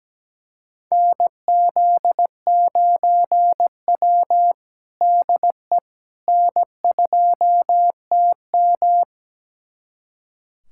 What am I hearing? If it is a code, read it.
Morse code, heard as a keyed tone, about 17 wpm, 700 Hz. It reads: NZ9W DE N2TM
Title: USA Callsign DE Callsign - 4 Characters - 17wpm